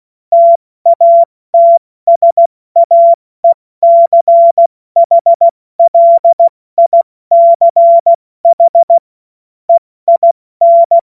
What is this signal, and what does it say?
Morse code, heard as a keyed tone, about 16 wpm, 680 Hz. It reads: TATSAECHLICH EIN
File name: Morsekurs / Witze / Witz 29. Juni 2021, 70 F80